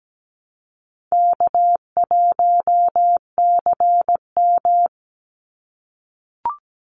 Morse code, heard as a keyed tone, about 17 wpm, 700 Hz. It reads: K1CM
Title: NA SOTA Chaser Callsigns 17wpm